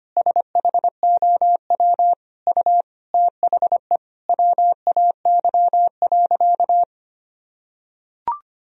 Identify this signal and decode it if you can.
Morse code, heard as a keyed tone, about 25 wpm, 700 Hz. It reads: SHOW U THE WAY.
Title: Sentences from Top 300 Words and Abbreviations 25wpm